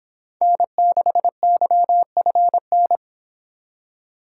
Morse code, heard as a keyed tone, about 25 wpm, 700 Hz. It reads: N6YFN